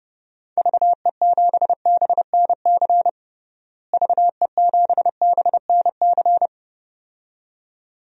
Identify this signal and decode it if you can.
Morse code, heard as a keyed tone, about 30 wpm, 700 Hz. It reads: VE7BNC VE7BNC
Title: DXCC-100 Call Signs - 6 Characters - Repeated 2x - 30wpm